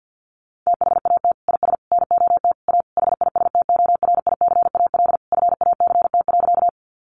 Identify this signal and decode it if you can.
Morse code, heard as a keyed tone, about 50 wpm, 700 Hz. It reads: THAT IS NOT A HISTORICAL FACT.
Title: Sentences from Top 600 Words 50wpm